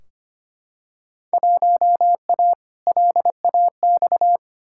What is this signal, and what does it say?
Morse code, heard as a keyed tone, about 25 wpm, 700 Hz. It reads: 1A LAX
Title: Field Day Exchanges - 25wpm